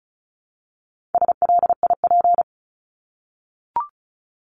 Morse code, heard as a keyed tone, about 34 wpm, 700 Hz. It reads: SLIP
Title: Top 1751-2000 Words 35wpm